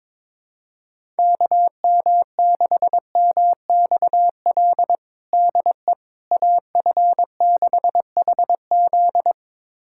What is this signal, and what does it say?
Morse code, heard as a keyed tone, about 22 wpm, 700 Hz. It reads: KM6MXL DE AF6HZ
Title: USA Callsign DE Callsign 22wpm